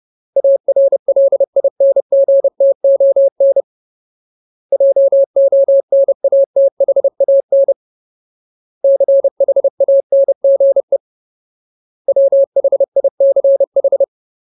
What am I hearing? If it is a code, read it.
Morse code, heard as a keyed tone, about 30 wpm, 550 Hz. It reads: ARLINGTON JONATHAN CHANGE WHICH
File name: Ultimate Word List / 30WPM Ultimate